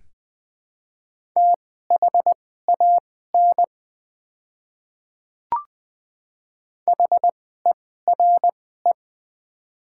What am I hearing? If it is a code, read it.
Morse code, heard as a keyed tone, about 20 wpm, spaced as for 14 wpm, 700 Hz. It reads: THAN HERE
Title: 2x Character Spacing - Top 300 Words - 20wpm